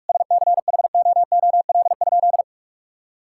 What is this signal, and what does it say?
Morse code, heard as a keyed tone, about 46 wpm, 700 Hz. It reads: SCHOOL?
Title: Sentences from Top 1000 Words - Review Part 6 - 45wpm